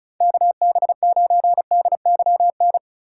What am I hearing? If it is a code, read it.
Morse code, heard as a keyed tone, about 36 wpm, 700 Hz. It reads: KB9DYN
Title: USA Call Signs - 6 Characters - 4x Character Spacing - 35wpm